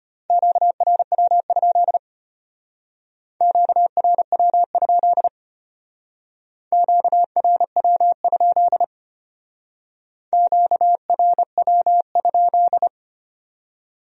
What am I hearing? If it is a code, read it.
Morse code, heard as a keyed tone, about 32 wpm, 700 Hz. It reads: QRW? QRW? QRW? QRW?